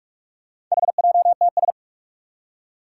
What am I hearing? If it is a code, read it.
Morse code, heard as a keyed tone, about 44 wpm, 700 Hz. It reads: HJTS
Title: Four Challenging Letter-Number Combinations 45wpm